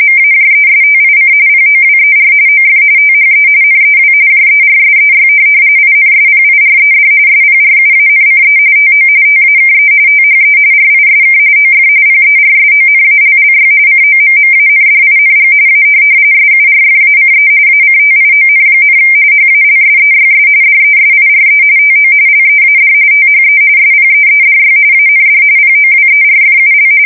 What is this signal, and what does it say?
RTTY, about 45.45 baud. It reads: MAJESTIC POWER OF THE NEW JERSEY INSPIRED SPRUANCE TO TAKE THE RISK OF PERSONALLY COMMANDING A SURFACE ACTION, HALSEYS DESIRE TO LEAD THE SPEEDY AND POWERFUL
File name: fsk170 / USS-New-Jersey